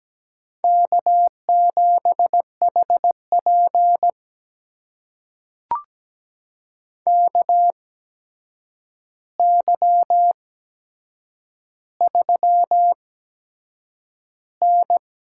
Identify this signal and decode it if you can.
Morse code, heard as a keyed tone, about 17 wpm, spaced as for 7 wpm, 700 Hz. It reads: K7HP KY3N
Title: USA Call Signs - 4 Characters - 8x Character Spacing - 17wpm